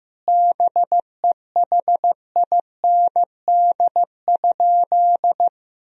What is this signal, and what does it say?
Morse code, heard as a keyed tone, about 15 wpm, 700 Hz. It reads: BEHIND?
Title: Sentences from Top 1000 Words - Review Part 2 - 15wpm